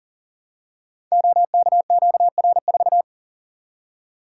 Morse code, heard as a keyed tone, about 39 wpm, 700 Hz. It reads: OKQR4
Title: Code Groups - 5 Letter:Numbers - 40wpm